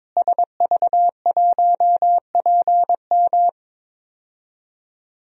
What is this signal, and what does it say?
Morse code, heard as a keyed tone, about 22 wpm, 700 Hz. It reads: SV1PM
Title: DXCC-100 Call Signs - 5 Characters - Repeated 3x - 22wpm